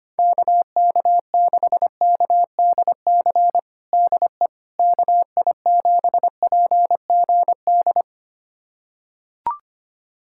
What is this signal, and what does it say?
Morse code, heard as a keyed tone, about 25 wpm, 700 Hz. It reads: KK6KDC DE KI7PGD
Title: USA Callsign DE Callsign 25wpm